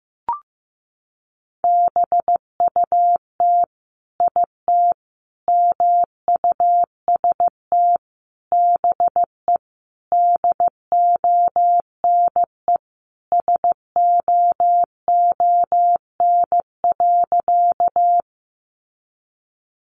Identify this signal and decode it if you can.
Morse code, heard as a keyed tone, about 15 wpm, 700 Hz. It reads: BUT IT MUST BE DONE SOON.
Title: Sentences from Top 500 Words 15wpm